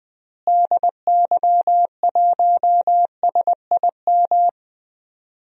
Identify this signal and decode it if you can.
Morse code, heard as a keyed tone, about 20 wpm, 700 Hz. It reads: DY1SIM
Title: DXCC-100 Call Signs - 6 Characters - Repeated 3x - 20wpm